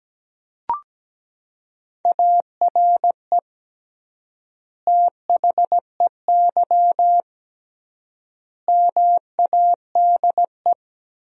Morse code, heard as a keyed tone, about 17 wpm, 700 Hz. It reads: ARE THEY MADE
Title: 3x Word-Spacing - Sentences from Top 400 Words - 17wpm